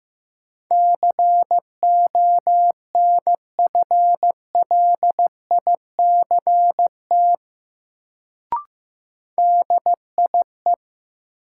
Morse code, heard as a keyed tone, about 15 wpm, 700 Hz. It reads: CONFLICT DIE
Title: Sentences from Top 1300 Words - New Word Drill - 15wpm